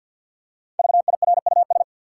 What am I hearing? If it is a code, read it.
Morse code, heard as a keyed tone, about 50 wpm, 700 Hz. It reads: VIRUS
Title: Five Letter Words - 50wpm